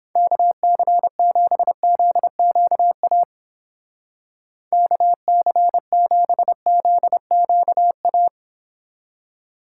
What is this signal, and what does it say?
Morse code, heard as a keyed tone, about 29 wpm, 700 Hz. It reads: KC7ZQA KC7ZQA